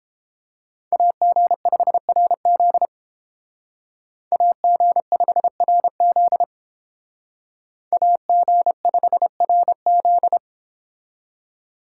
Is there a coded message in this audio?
Yes: Morse code from 0.9 to 10.4 s, heard as a keyed tone, about 30 wpm, 700 Hz. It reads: AG5RZ AG5RZ AG5RZ